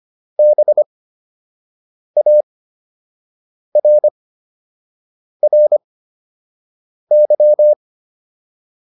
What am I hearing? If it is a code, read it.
Morse code, heard as a keyed tone, about 25 wpm, spaced as for 6 wpm, 600 Hz. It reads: BARRY